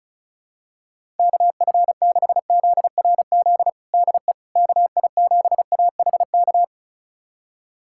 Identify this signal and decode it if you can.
Morse code, heard as a keyed tone, about 36 wpm, 700 Hz. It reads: KF6ZRZ DE KI7AHK
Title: USA Callsign DE Callsign 35wpm